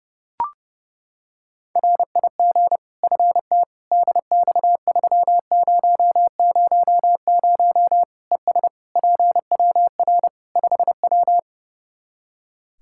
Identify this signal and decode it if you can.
Morse code, heard as a keyed tone, about 30 wpm, 700 Hz. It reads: RIG FT DX3000 ES PWR 5W
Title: Practice QSO - Exchange 2 - Info - 30wpm